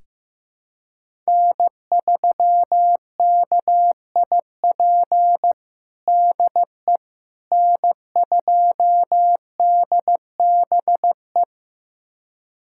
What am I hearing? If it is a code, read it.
Morse code, heard as a keyed tone, about 15 wpm, 700 Hz. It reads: N3KIP DE N2DBE
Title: USA Callsign DE Callsign - 5 Characters - 15wpm